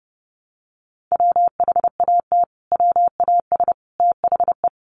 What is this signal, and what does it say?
Morse code, heard as a keyed tone, about 30 wpm, 700 Hz. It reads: WHAT WAS THE
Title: Sentences from Top 1300 Words 30wpm